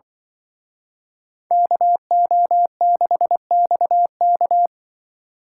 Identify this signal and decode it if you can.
Morse code, heard as a keyed tone, about 24 wpm, 700 Hz. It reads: KO6XK
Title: Speed Racing - USA 2x2 Call Signs 20wpm